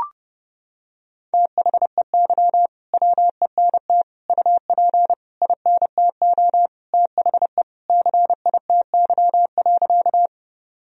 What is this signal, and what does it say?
Morse code, heard as a keyed tone, about 31 wpm, 700 Hz. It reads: THEY WENT UP INTO THE CITY.
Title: Sentences from Top 1000 Words - Review Part 1 - 30wpm